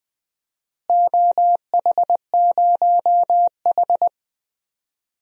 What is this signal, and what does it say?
Morse code, heard as a keyed tone, about 20 wpm, 700 Hz. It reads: OH0H
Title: DXCC-100 Call Signs - 4 Characters - Repeated 2x - 20wpm